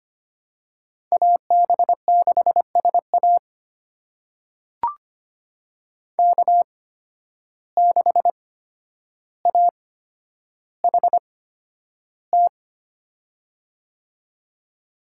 Morse code, heard as a keyed tone, about 25 wpm, spaced as for 11 wpm, 700 Hz. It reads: AB6SA K6AHT
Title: USA Call Signs - 5 Characters - 8x Character Spacing - 25wpm